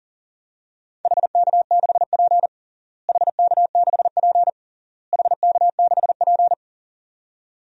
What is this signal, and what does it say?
Morse code, heard as a keyed tone, about 37 wpm, 700 Hz. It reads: HK6P HK6P HK6P
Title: DXCC-100 Call Signs - 4 Characters - Repeated 3x - 40wpm